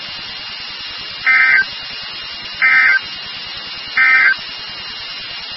Emergency Alert System (SAME) tones, heard as harsh buzzing bursts. It is heard from 1.1 to 4.4 s.
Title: NOAA Weather Radio WXK36